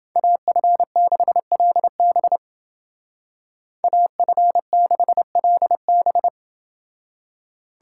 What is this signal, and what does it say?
Morse code, heard as a keyed tone, about 28 wpm, 700 Hz. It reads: AF6LB AF6LB